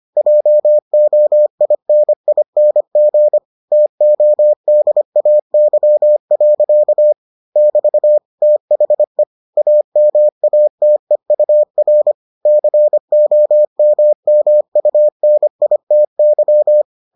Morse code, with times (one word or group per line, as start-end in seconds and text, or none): 0.2-3.4 JOINING
3.7-7.1 TODAY.
7.6-8.2 <BT>
8.4-9.2 THE
9.6-12.1 AMATEUR
12.5-16.8 COMMUNITY